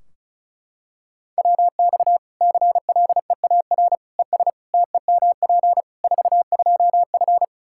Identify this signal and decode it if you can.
Morse code, heard as a keyed tone, about 36 wpm, 700 Hz. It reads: WX CLEAR ES TEMP 42F
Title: Practice QSO - Exchange 2 - Info - 35wpm